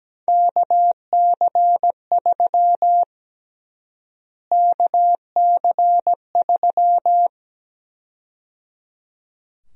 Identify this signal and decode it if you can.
Morse code, heard as a keyed tone, about 17 wpm, 700 Hz. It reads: KC3 KC3
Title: USA Call Sign Prefixes - Repeated 2x - 17wpm